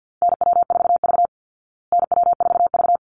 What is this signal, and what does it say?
Morse code, heard as a keyed tone, about 50 wpm, 700 Hz. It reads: NW4V NW4V
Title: Call Signs Repeated 3x - 50wpm